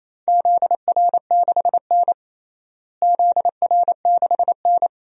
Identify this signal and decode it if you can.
Morse code, heard as a keyed tone, about 28 wpm, 700 Hz. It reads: ZR6N ZR6N
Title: DXCC-100 Call Signs - 4 Characters - Repeated 3x - 28wpm